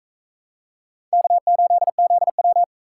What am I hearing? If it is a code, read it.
Morse code, heard as a keyed tone, about 35 wpm, 700 Hz. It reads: K8ZW